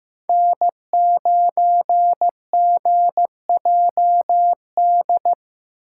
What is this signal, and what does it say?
Morse code, heard as a keyed tone, about 16 wpm, 700 Hz. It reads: N9GJD
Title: USA Call Signs - 5 Characters - 4x Character Spacing - 15wpm